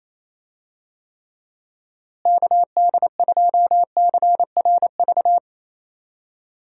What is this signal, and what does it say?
Morse code, heard as a keyed tone, about 28 wpm, 700 Hz. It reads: KD2CRV
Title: USA Call Signs - 6 Characters - Repeated 2x - 28wpm